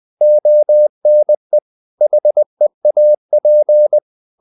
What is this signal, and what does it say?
Morse code, heard as a keyed tone, about 20 wpm, 600 Hz. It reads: ONE HEAP
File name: CW / Practice / If20